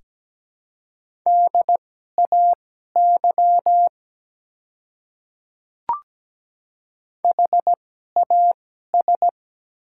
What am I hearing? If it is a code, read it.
Morse code, heard as a keyed tone, about 17 wpm, spaced as for 12 wpm, 700 Hz. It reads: DAY HAS
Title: 2x Character Spacing - Top 100 Words - 17wpm